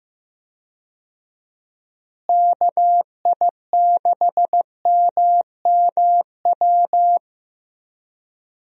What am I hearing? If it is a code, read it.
Morse code, heard as a keyed tone, about 16 wpm, 700 Hz. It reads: KI6MMW
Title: USA Call Signs - 6 Characters - Repeated 2x - 15wpm